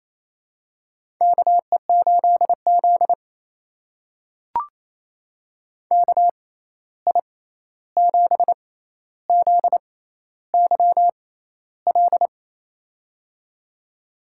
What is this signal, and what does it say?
Morse code, heard as a keyed tone, about 28 wpm, spaced as for 13 wpm, 700 Hz. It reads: KE8Z KI7ZYL
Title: USA Call Signs - 6x Character Spacing - 28wpm